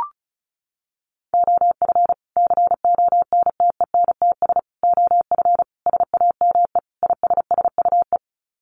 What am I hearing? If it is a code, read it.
Morse code, heard as a keyed tone, about 34 wpm, 700 Hz. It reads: OF CONTENTS OF SAME ISSUE
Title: Sets of 5 Words Encore2 35wpm